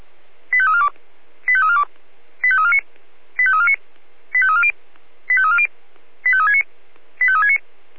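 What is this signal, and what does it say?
Five-tone selective call (ZVEI), heard as quick runs of five tones, tones of about 80 ms. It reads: 85342 85342 85349 85349 85340 85340 85369 85369
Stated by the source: ZVEI five-tone selective call 85360, received via SDR.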